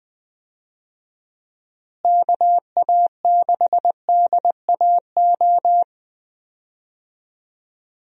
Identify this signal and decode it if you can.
Morse code, heard as a keyed tone, about 21 wpm, 700 Hz. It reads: KA6DAO
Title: USA Call Signs - 6 Characters - Repeated 2x - 20wpm